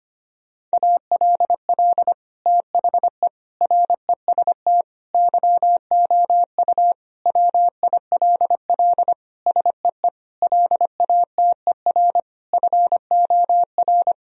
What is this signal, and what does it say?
Morse code, heard as a keyed tone, about 25 wpm, 700 Hz. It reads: ALL THE REST YOU WILL SEE LATER FOR
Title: Sentences from Top 1000 Words - Review Part 4 - 25wpm